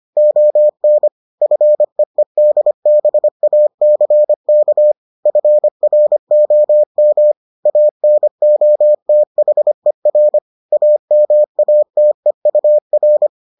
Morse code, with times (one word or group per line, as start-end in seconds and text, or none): 0.2-1.1 ON
1.4-4.9 FEEDBACK
5.3-7.3 FROM
7.7-10.4 ANOTHER
10.7-13.3 AMATEUR